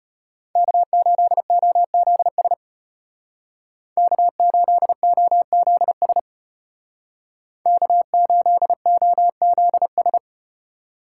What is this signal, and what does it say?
Morse code, heard as a keyed tone, about 35 wpm, 700 Hz. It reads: K8OZS K8OZS K8OZS